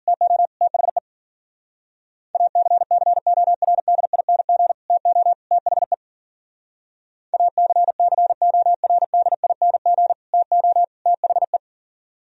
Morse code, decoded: TO THE ACCORDING TO THE ACCORDING TO THE

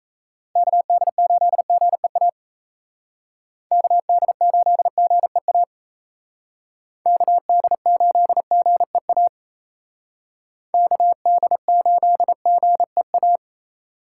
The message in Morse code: KD8GEA KD8GEA KD8GEA KD8GEA

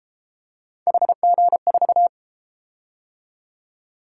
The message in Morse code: HG4

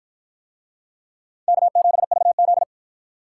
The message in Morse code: K7VZ

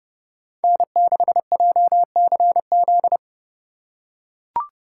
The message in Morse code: N6JCZ

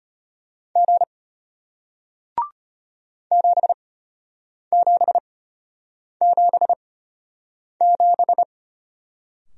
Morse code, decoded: G 7777

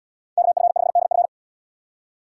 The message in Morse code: KF5DF

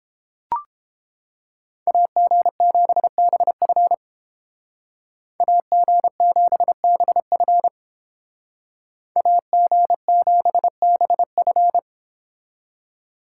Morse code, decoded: AG7BF AG7BF AG7BF

